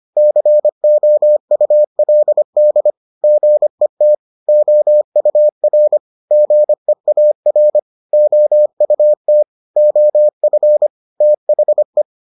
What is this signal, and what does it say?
Morse code, heard as a keyed tone, about 25 wpm, 600 Hz. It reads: COULD GET OUR GEAR OUT OF THE